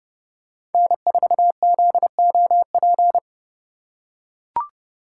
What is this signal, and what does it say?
Morse code, heard as a keyed tone, about 32 wpm, 700 Hz. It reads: N4ZOP
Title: USA Call Signs - 5 Characters - 3x Character Spacing - 30wpm